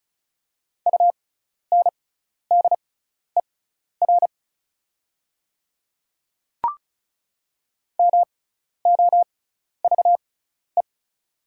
Morse code, heard as a keyed tone, about 36 wpm, spaced as for 12 wpm, 700 Hz. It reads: UNDER MOVE
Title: 6x Character Spacing - Top 300 Words - 35wpm